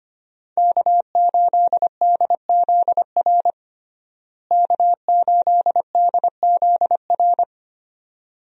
Morse code, heard as a keyed tone, about 25 wpm, 700 Hz. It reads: K8DZR K8DZR
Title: USA Call Signs - 5 Characters - Repeated 2x - 25wpm